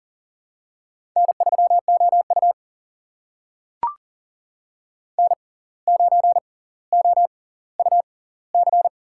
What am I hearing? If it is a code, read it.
Morse code, heard as a keyed tone, about 40 wpm, spaced as for 19 wpm, 700 Hz. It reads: N3OU N9OUC
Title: USA Call Signs - 6x Character Spacing - 40wpm